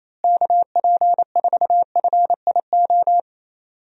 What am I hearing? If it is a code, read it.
Morse code, heard as a keyed tone, about 28 wpm, 700 Hz. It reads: KP4FIO
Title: USA Call Signs - 6 Characters - Repeated 3x - 28wpm